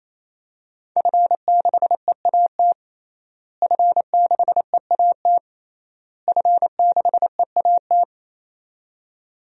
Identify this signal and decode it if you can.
Morse code, heard as a keyed tone, about 28 wpm, 700 Hz. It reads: F6EAT F6EAT F6EAT
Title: DXCC-100 Call Signs - 5 Characters - Repeated 3x - 28wpm